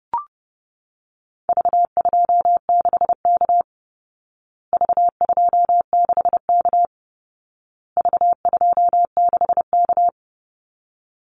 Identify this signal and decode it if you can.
Morse code, heard as a keyed tone, about 28 wpm, 700 Hz. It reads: V26K V26K V26K
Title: Call Signs Repeated 2x - 30wpm